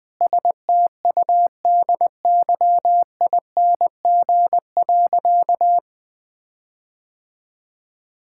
Morse code, heard as a keyed tone, about 21 wpm, 700 Hz. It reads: STUDYING.
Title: Sentences with 6-Word Phrases - 20wpm